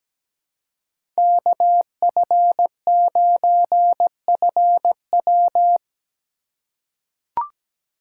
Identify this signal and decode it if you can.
Morse code, heard as a keyed tone, about 17 wpm, 700 Hz. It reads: KF9FW
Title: USA Call Signs - 2x Character Spacing - 17wpm